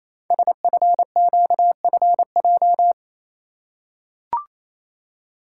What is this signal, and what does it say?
Morse code, heard as a keyed tone, about 27 wpm, 700 Hz. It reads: SFQFJ
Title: Code Groups - 5 Letters - 28wpm